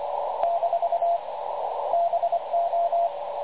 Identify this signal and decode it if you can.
Morse code, heard as a keyed tone, about 24 wpm, 680 Hz, with radio noise. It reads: - BO
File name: dlazaawansowanych / nagraniazpasma / websdr recording 3567.0kHz